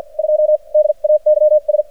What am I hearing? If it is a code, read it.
Morse code, heard as a keyed tone, about 54 wpm, spaced as for 43 wpm, 600 Hz. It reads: 2 DAYS
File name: archived / CRSnet